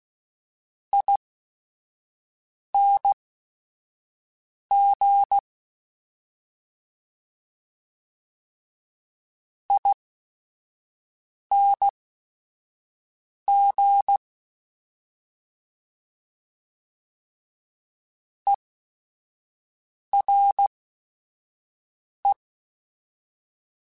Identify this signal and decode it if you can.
Morse code, heard as a keyed tone, about 16 wpm, spaced as for 5 wpm, 800 Hz. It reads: ING ING ERE